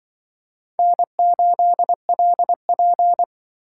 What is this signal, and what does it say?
Morse code, heard as a keyed tone, about 25 wpm, 700 Hz. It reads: N8LP
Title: Speed Racing - USA 1x2 Call Signs 20wpm